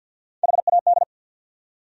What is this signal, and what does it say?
Morse code, heard as a keyed tone, about 67 wpm, spaced as for 58 wpm, 700 Hz. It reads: HAD
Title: Speed-Racing Top 100 Words 50wpm